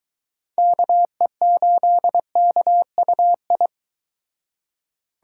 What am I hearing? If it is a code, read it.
Morse code, heard as a keyed tone, about 22 wpm, 700 Hz. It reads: KE8KUI